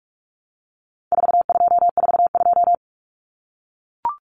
Morse code, heard as a keyed tone, about 44 wpm, 700 Hz. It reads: 4242
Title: Numbers - 4-Digit 45wpm